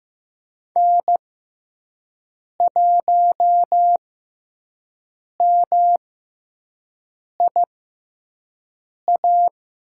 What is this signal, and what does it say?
Morse code, heard as a keyed tone, about 15 wpm, spaced as for 5 wpm, 700 Hz. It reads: N1MIA
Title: USA Call Signs - 5 Characters - 6x Character Spacing - 15wpm